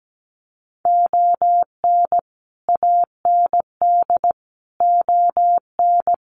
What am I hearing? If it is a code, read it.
Morse code, heard as a keyed tone, about 17 wpm, 700 Hz. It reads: ON AND ON
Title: Sets of 5 Words Encore2 17wpm